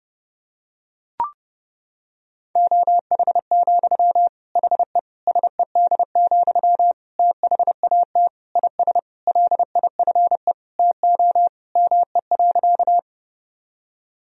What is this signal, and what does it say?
Morse code, heard as a keyed tone, about 30 wpm, 700 Hz. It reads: OH, HE SED, THAT IS LIFE TO ME.
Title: Sentences from Top 500 Words and Abbreviations 30wpm